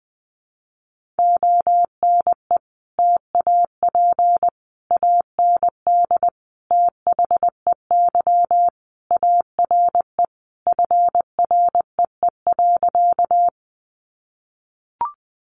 Morse code, heard as a keyed tone, about 20 wpm, 700 Hz. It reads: ONE TAP AND THEY ARE FREE.
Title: Sentences from Top 1400 Words 20wpm